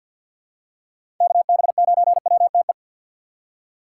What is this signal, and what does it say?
Morse code, heard as a keyed tone, about 50 wpm, 700 Hz. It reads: KB9WTE
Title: USA Call Signs - 6 Characters - 3x Character Spacing - 50wpm